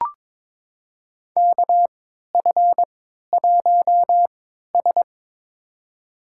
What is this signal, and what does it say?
Morse code, heard as a keyed tone, about 22 wpm, spaced as for 12 wpm, 700 Hz. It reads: KF1S
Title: USA Call Signs - 4 Characters - 3x Character Spacing - 22wpm